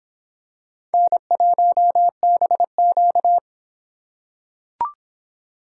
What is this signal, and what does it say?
Morse code, heard as a keyed tone, about 24 wpm, 700 Hz. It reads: N1BQ